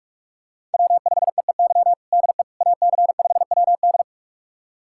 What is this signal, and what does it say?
Morse code, heard as a keyed tone, about 46 wpm, 700 Hz. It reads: W5EEY DE AC5WD
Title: USA Callsign DE Callsign - 5 Characters - 45wpm